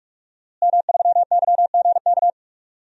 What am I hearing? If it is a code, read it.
Morse code, heard as a keyed tone, about 46 wpm, 700 Hz. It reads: M3YGK
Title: DXCC-100 Call Signs - 5 Characters - Repeated 2x - 45wpm